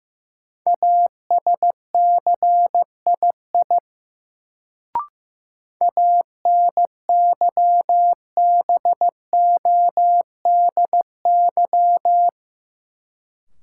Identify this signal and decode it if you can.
Morse code, heard as a keyed tone, about 16 wpm, 700 Hz. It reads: ASCII ANYBODY
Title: Sets of 5 Words - New Word Drill - 15wpm